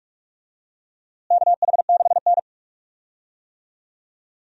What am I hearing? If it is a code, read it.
Morse code, heard as a keyed tone, about 44 wpm, 700 Hz. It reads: KH6N
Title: DXCC-100 Call Signs - 4 Characters - Repeated 2x - 45wpm